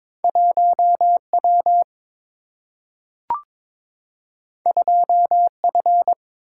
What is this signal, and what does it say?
Morse code, heard as a keyed tone, about 22 wpm, 700 Hz. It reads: 1 W 2 F
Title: Single-Letter Single-Number 22wpm